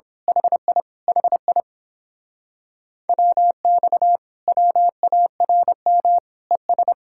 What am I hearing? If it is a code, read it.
Morse code, heard as a keyed tone, about 26 wpm, 700 Hz. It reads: HI HI WX WARM ES
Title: Speed-Racing Common QSO Phrases - 22wpm